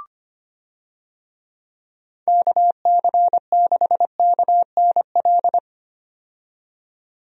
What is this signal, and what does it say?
Morse code, heard as a keyed tone, about 25 wpm, 700 Hz. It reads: KC6KNL
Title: USA Call Signs - 6 Characters - 25wpm